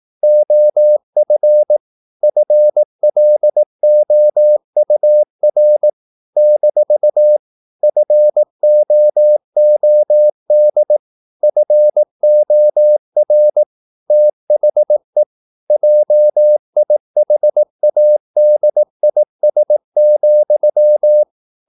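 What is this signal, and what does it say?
Morse code, heard as a keyed tone, about 18 wpm, 600 Hz. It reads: OF FLOUR - FOOD FOR THE JIHADIS,
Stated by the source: Morse practice at 18 wpm, English text, from CW med Gnister.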